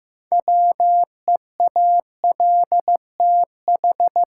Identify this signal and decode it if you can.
Morse code, heard as a keyed tone, about 15 wpm, 700 Hz. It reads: WEALTH
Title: Sentences from Top 1200 Words - New Word Drill Alt - 15wpm